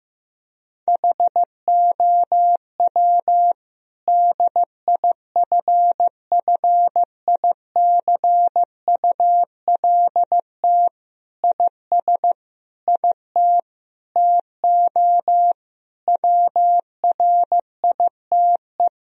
Morse code, heard as a keyed tone, about 14 wpm, 700 Hz. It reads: HOW DIFFICULT IS IT TO WRITE
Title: Sentences from Top 1000 Words - Review Part 3 - 15wpm